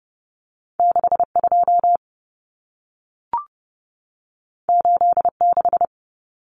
Morse code, heard as a keyed tone, about 30 wpm, 700 Hz. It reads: 62 86
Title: Numbers - 2-Digit 30wpm